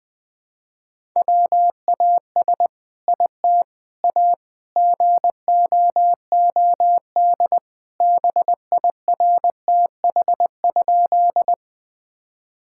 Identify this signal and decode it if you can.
Morse code, heard as a keyed tone, about 20 wpm, 700 Hz. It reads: WAS IT A GOOD BIRTH?
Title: Sentences from Top 1000 Words - Review Part 5 - 20wpm